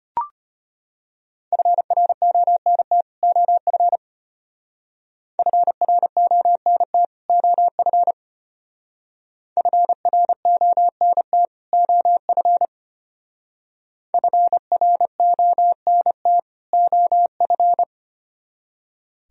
Morse code, heard as a keyed tone, about 31 wpm, 700 Hz. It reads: FRONT OF FRONT OF FRONT OF FRONT OF
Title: Speed-Racing - Sets of 2 Words 25wpm